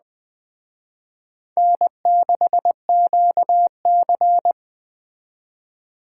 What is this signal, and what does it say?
Morse code, heard as a keyed tone, about 21 wpm, 700 Hz. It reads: N6QC